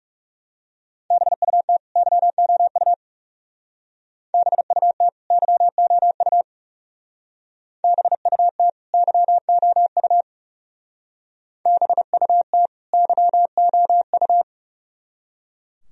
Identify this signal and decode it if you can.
Morse code, heard as a keyed tone, about 37 wpm, 700 Hz. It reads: BUT YOU BUT YOU BUT YOU BUT YOU
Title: Speed-Racing - Sets of 2 Words 30wpm